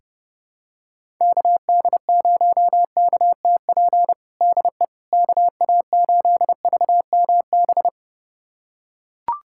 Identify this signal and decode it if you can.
Morse code, heard as a keyed tone, about 32 wpm, 700 Hz. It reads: KD0KTP DE KA8VMB
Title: USA Callsign DE Callsign 30wpm